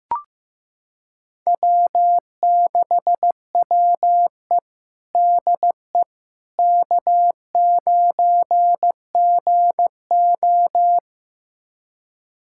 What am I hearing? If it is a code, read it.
Morse code, heard as a keyed tone, about 15 wpm, 700 Hz. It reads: W6WE DE K9GO
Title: USA Callsign DE Callsign - 4 Characters - 15wpm